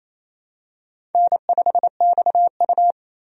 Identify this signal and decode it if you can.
Morse code, heard as a keyed tone, about 28 wpm, 700 Hz. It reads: N5XU
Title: USA Call Signs - 4 Characters - 28wpm